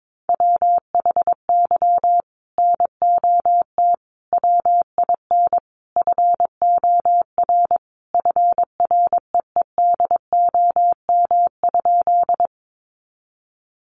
Morse code, heard as a keyed tone, about 22 wpm, 700 Hz. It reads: WHY NOT WIN FOR FREEDOM?
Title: Sentences from Top 1500 Words 22wpm